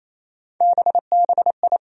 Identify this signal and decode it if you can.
Morse code, heard as a keyed tone, about 28 wpm, 700 Hz. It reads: BBI
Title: USA Call Sign Postfixes - 3 Letters 28wpm